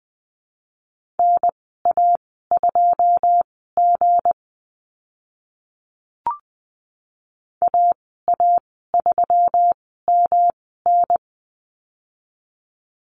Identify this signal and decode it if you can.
Morse code, heard as a keyed tone, about 20 wpm, spaced as for 15 wpm, 700 Hz. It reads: NA2G AA3MN
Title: 245 Lesson 39g - Letter z - Call Signs 20wpm